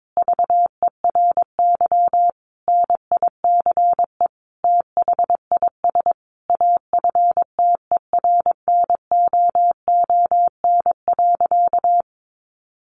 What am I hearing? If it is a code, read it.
Morse code, heard as a keyed tone, about 22 wpm, 700 Hz. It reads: VERY NICE THIS AFTERNOON.
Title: Sentences from Top 1500 Words 22wpm